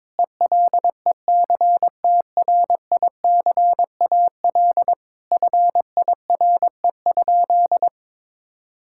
Morse code, heard as a keyed tone, about 22 wpm, 700 Hz. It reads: ELECTRICAL FIRE?
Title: Sentences from Top 1000 Words - Review Part 6 - 22wpm